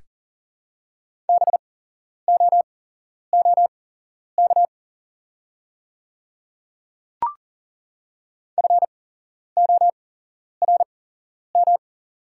Morse code, heard as a keyed tone, about 40 wpm, spaced as for 11 wpm, 700 Hz. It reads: BOOK FORM
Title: 8x Character Spacing - Top 300 Words - 40wpm